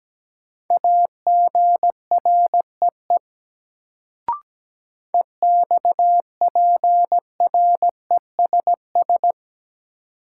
Morse code, heard as a keyed tone, about 17 wpm, 700 Hz. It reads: AGREE EXPRESS
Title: Sentences from Top 1100 Words - New Word Drill Alt - 17wpm